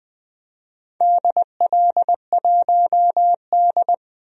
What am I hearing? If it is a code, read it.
Morse code, heard as a keyed tone, about 20 wpm, 700 Hz. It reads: DL1D
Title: DXCC-100 Call Signs - 4 Characters - 20wpm